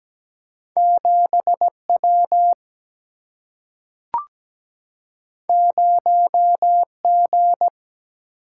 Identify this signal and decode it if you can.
Morse code, heard as a keyed tone, about 17 wpm, 700 Hz. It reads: 7 W 0 G
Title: Single-Letter Single-Number 17wpm